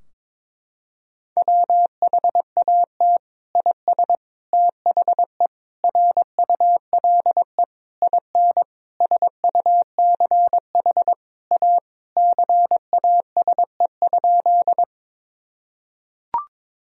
Morse code, heard as a keyed tone, about 22 wpm, 700 Hz. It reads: WHAT IS THE RULE IN SUCH A CASE?
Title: Sentences from Top 1000 Words - Review Part 2 - 22wpm